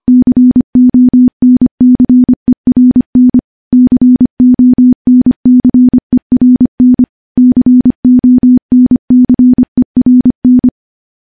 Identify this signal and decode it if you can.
Morse code, heard as a keyed tone, about 25 wpm, 250 Hz. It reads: CONCERN CONCERN CONCERN
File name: morse / morse-training-25WPM